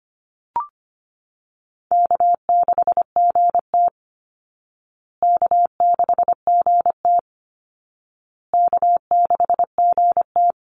Morse code, heard as a keyed tone, about 25 wpm, 700 Hz. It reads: K6GT K6GT K6GT
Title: Call Signs Repeated 2x - 25wpm